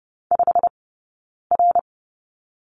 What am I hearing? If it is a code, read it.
Morse code, heard as a keyed tone, about 30 wpm, 700 Hz. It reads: 5 R
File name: Lesson 15d - Number 5 - Two Characters 30wpm